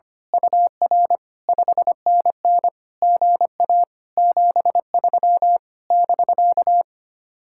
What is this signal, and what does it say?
Morse code, heard as a keyed tone, about 24 wpm, 700 Hz. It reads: UR 5NN GA 73 <BK>
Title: Speed-Racing Common QSO Phrases Encore - 25wpm